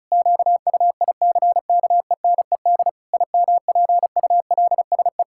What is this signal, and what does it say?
Morse code, heard as a keyed tone, about 34 wpm, 700 Hz. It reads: QUICKENED IMPULSE
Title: Pangram Challenge v2 - 35wpm